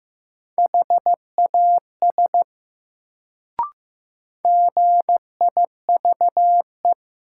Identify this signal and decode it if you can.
Morse code, heard as a keyed tone, about 15 wpm, 700 Hz. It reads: HAS GIVE
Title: Sets of 2 Words - New Word Drill - 15wpm